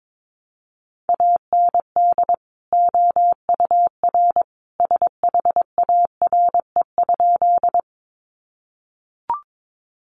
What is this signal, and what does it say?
Morse code, heard as a keyed tone, about 22 wpm, 700 Hz. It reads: AND OUR SHARE?
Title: Sentences from Top 900 Words 22wpm